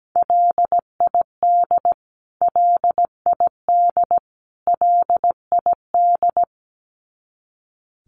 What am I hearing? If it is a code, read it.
Morse code, heard as a keyed tone, about 17 wpm, 700 Hz. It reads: LID LID LID
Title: Abbreviations Repeated 3x - 17wpm